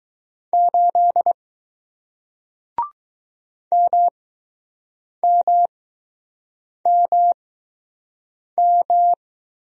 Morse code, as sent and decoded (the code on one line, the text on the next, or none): ---.. / -- -- -- --
8 MMMM